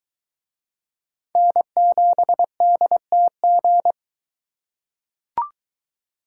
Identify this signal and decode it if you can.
Morse code, heard as a keyed tone, about 23 wpm, 700 Hz. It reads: N7DTG